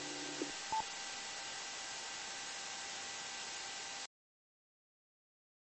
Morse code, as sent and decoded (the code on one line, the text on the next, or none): none